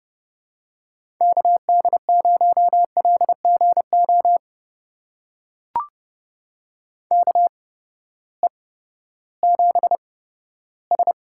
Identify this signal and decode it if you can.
Morse code, heard as a keyed tone, about 31 wpm, spaced as for 15 wpm, 700 Hz. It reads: KD0LGO KE7S
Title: USA Call Signs - 6 Characters - 8x Character Spacing - 30wpm